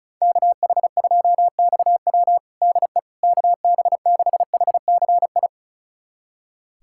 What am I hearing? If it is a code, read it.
Morse code, heard as a keyed tone, about 34 wpm, 700 Hz. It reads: KH2XW DE KB6HCI
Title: USA Callsign DE Callsign 35wpm